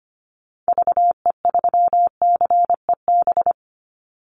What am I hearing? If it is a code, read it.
Morse code, heard as a keyed tone, about 25 wpm, 700 Hz. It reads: VE3CEB
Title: Call Signs Repeated 2x - 25wpm